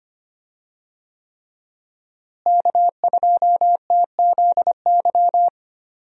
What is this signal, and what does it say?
Morse code, heard as a keyed tone, about 25 wpm, 700 Hz. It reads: K2TZY
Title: USA Call Signs - 5 Characters - Repeated 3x - 25wpm